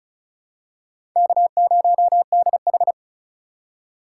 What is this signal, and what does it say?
Morse code, heard as a keyed tone, about 36 wpm, 700 Hz. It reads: K0DH
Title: USA Call Signs - 4x Character Spacing - 35wpm